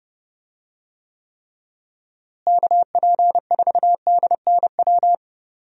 Morse code, heard as a keyed tone, about 30 wpm, 700 Hz. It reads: KP4DNW
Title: USA Call Signs - 6 Characters - Repeated 3x - 30wpm